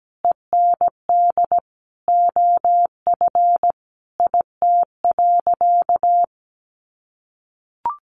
Morse code, heard as a keyed tone, about 17 wpm, 700 Hz. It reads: END OF IT.
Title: Sentences from Top 1200 Words 17wpm